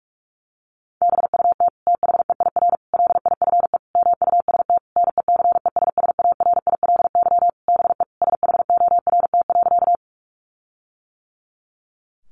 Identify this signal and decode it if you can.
Morse code, heard as a keyed tone, about 44 wpm, 700 Hz. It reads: BUT THEIR LIFE MUST NECESSARILY BE SHORT.
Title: Sentences from Top 1400 Words 45wpm